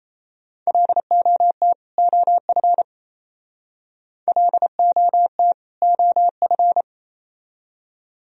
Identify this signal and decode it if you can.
Morse code, heard as a keyed tone, about 35 wpm, 700 Hz. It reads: LOT OF LOT OF
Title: Speed-Racing - Sets of 2 Words 28wpm